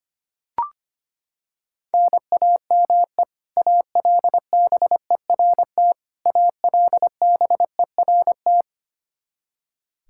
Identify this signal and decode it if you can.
Morse code, heard as a keyed tone, about 25 wpm, 700 Hz. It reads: NAME ALBERT ALBERT
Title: Practice QSO - Exchange 1 - Beginning - 25wpm